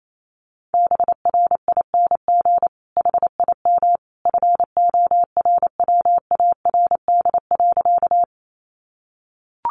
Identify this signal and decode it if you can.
Morse code, heard as a keyed tone, about 28 wpm, 700 Hz. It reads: BRING HIM FORWARD.
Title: Sentences from Top 1300 Words 28wpm